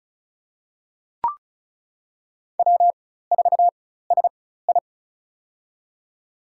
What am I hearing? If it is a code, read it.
Morse code, heard as a keyed tone, about 34 wpm, spaced as for 16 wpm, 700 Hz. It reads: W4SI
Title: USA Call Signs - 4 Characters - 4x Character Spacing - 35wpm